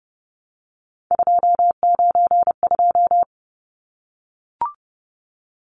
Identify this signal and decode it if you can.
Morse code, heard as a keyed tone, about 30 wpm, 700 Hz. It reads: 292
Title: Numbers - 3-Digit 30wpm